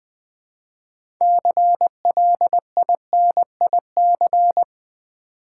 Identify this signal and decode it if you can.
Morse code, heard as a keyed tone, about 20 wpm, 700 Hz. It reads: CLINIC
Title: 6 Letter Words - First 3 Letters Given - 20wpm